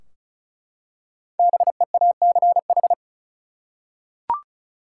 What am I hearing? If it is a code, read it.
Morse code, heard as a keyed tone, about 36 wpm, 700 Hz. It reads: BEACH
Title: Five Letter Words - 35wpm